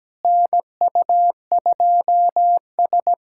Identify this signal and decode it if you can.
Morse code, heard as a keyed tone, about 17 wpm, 700 Hz. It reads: NU2S
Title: USA Callsign DE Callsign - 4 Characters - 17wpm